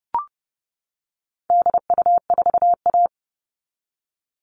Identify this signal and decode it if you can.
Morse code, heard as a keyed tone, about 30 wpm, 700 Hz. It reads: DU4A
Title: All Cut Numbers - 4-Digit 30wpm